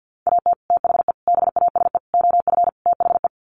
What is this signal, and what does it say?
Morse code, heard as a keyed tone, about 50 wpm, 700 Hz. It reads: AT THE BASE OF THE
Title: Sets of 5 Words 50wpm